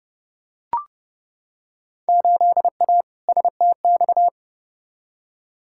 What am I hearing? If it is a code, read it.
Morse code, heard as a keyed tone, about 30 wpm, 700 Hz. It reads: 8A STX
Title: Field Day Exchanges Encore - 30wpm